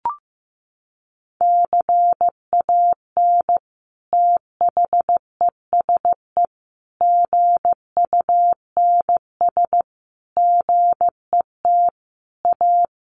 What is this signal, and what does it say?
Morse code, heard as a keyed tone, about 15 wpm, 700 Hz. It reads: CAN THESE GUNS GET A
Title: A Sound Of Thunder 15wpm